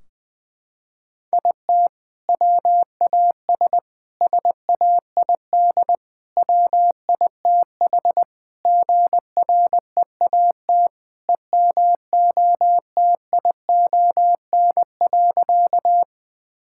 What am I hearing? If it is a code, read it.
Morse code, heard as a keyed tone, about 20 wpm, 700 Hz. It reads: IT WAS SAID WITH GREAT EMOTION.
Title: Sentences from Top 1600 Words 20wpm